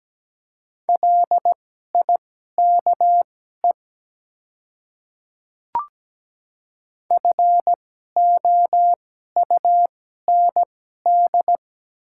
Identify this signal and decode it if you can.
Morse code, heard as a keyed tone, about 17 wpm, spaced as for 12 wpm, 700 Hz. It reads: LIKE FOUND
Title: 2x Character Spacing - Top 300 Words - 17wpm